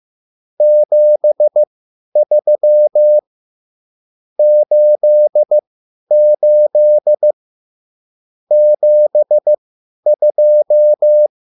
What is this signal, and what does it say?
Morse code, heard as a keyed tone, about 15 wpm, spaced as for 10 wpm, 600 Hz. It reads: 73 88 72
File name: QSL / CW lesson